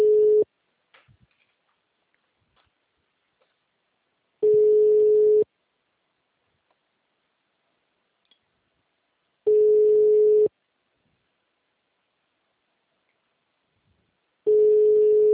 Ringback tone (Europe, 425 Hz).